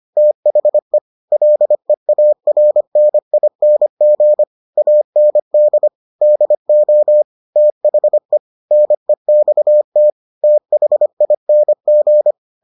Morse code, heard as a keyed tone, about 25 wpm, 600 Hz. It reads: THE LEARNING AND DO THE NEXT THING